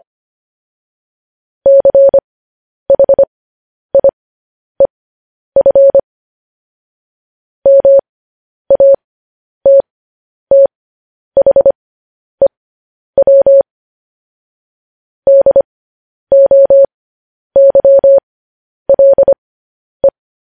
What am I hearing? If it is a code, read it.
Morse code, heard as a keyed tone, about 25 wpm, spaced as for 10 wpm, 550 Hz. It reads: CHIEF MATTHEW DOYLE
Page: news-headlines-in-morse-code-at-10-wpm